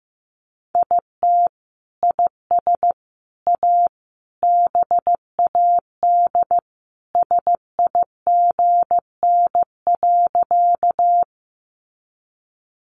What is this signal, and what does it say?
Morse code, heard as a keyed tone, about 15 wpm, 700 Hz. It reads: IT IS A BAD SIGN.
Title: Sentences from Top 1000 Words 15wpm